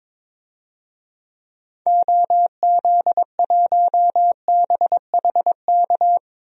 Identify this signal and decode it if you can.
Morse code, heard as a keyed tone, about 22 wpm, 700 Hz. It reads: OZ1BHK
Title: DXCC-100 Call Signs - 6 Characters - Repeated 3x - 22wpm